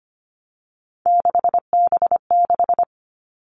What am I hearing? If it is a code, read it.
Morse code, heard as a keyed tone, about 25 wpm, 700 Hz. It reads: 6B6
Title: All Cut Numbers - 3-Digit 25wpm